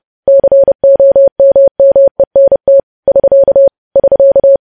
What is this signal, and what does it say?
Morse code, heard as a keyed tone, about 30 wpm, 550 Hz. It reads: COMMENT <SK> <SK>
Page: news headlines in morse code at 30 wpm